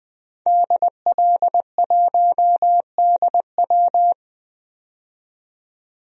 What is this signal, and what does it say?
Morse code, heard as a keyed tone, about 21 wpm, 700 Hz. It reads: DL1DW
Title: DXCC-100 Call Signs - 5 Characters - 20wpm